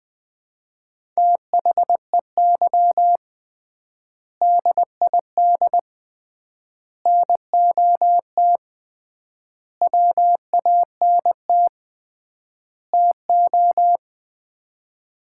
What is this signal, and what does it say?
Morse code, heard as a keyed tone, about 21 wpm, 700 Hz. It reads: THEY DID NOT WANT TO
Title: 3x Word-Spacing - Sets of 5 Words - 20wpm